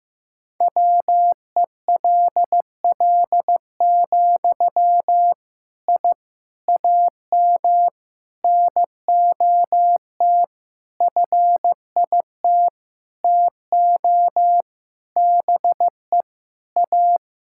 Morse code, heard as a keyed tone, about 15 wpm, 700 Hz. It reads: WELL, I AM NOT FIT TO BE A
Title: Sentences from Top 1000 Words - Review Part 5 - 15wpm